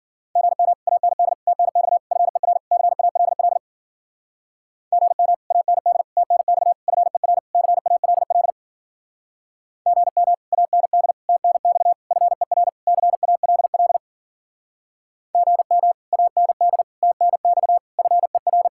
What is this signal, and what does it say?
Morse code, heard as a keyed tone, about 48 wpm, 700 Hz. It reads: GM AND TNX FER CALL GM AND TNX FER CALL GM AND TNX FER CALL GM AND TNX FER
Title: Speed-Racing Common QSO Phrases Encore - 40wpm